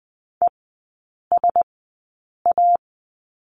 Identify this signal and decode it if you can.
Morse code, heard as a keyed tone, about 20 wpm, spaced as for 8 wpm, 700 Hz. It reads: ESA
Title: 035 Lesson 7e - Letter s - Three Characters 20wpm